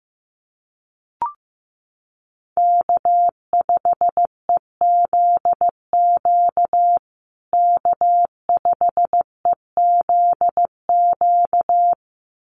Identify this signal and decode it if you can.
Morse code, heard as a keyed tone, about 15 wpm, 700 Hz. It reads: K5EZQ K5EZQ
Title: USA Call Signs Repeated 2x 15wpm